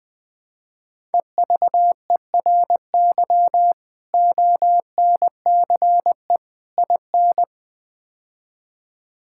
Morse code, heard as a keyed tone, about 20 wpm, 700 Hz. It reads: EVERY ONCE IN
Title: Build A Phrase - 5-Word Phrases - 20wpm